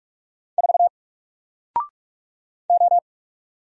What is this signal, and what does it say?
Morse code, heard as a keyed tone, about 44 wpm, 700 Hz. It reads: 4 O
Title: Single Letter-Number Alt 45wpm